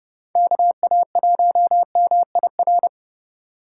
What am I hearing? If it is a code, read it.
Morse code, heard as a keyed tone, about 30 wpm, 700 Hz. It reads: KA1MIR